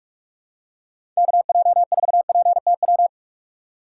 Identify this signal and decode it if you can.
Morse code, heard as a keyed tone, about 41 wpm, 700 Hz. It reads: KJ4PTW
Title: USA Call Signs - 6x Character Spacing - 45wpm